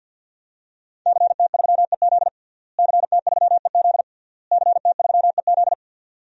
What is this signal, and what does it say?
Morse code, heard as a keyed tone, about 51 wpm, 700 Hz. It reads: CT3EZ CT3EZ CT3EZ
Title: DXCC-100 Call Signs - 5 Characters - Repeated 3x - 50wpm